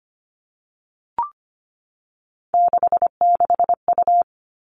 Morse code, heard as a keyed tone, about 25 wpm, 700 Hz. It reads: 66U
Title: All Cut Numbers - 3-Digit 25wpm